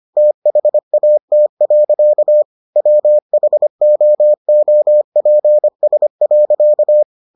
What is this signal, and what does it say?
Morse code, heard as a keyed tone, about 25 wpm, 600 Hz. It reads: THAT. WHOOPS.